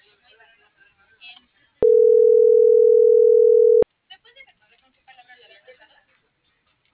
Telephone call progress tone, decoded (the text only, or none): ringback tone